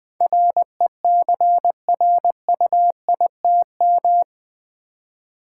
Morse code, heard as a keyed tone, about 20 wpm, 700 Hz. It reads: RECRUITM
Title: Build A Word - 4 Letters At A Time - 11-Letter Words - 20wpm